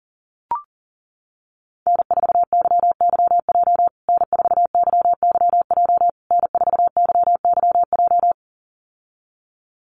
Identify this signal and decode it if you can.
Morse code, heard as a keyed tone, about 40 wpm, 700 Hz. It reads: N4YYJ N4YYJ N4YYJ
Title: USA Call Signs Repeated 3x 40wpm